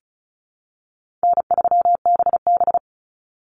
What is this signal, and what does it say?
Morse code, heard as a keyed tone, about 34 wpm, 700 Hz. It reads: N3BB
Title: Call Signs Repeated 3x - 35wpm